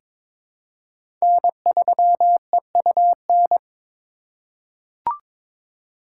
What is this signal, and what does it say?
Morse code, heard as a keyed tone, about 22 wpm, 700 Hz. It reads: N3EUN
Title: USA Call Signs - 5 Characters - 4x Character Spacing - 22wpm